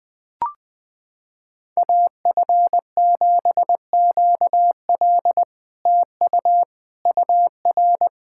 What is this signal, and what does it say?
Morse code, heard as a keyed tone, about 21 wpm, 700 Hz. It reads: AF7QL TU UR
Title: SOTA QSO Practice 20wpm